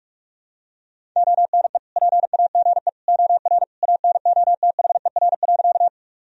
Morse code, heard as a keyed tone, about 46 wpm, 700 Hz. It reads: ONE PAGE OR ANOTHER.
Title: Sentences from Top 1000 Words - Review Part 1 - 45wpm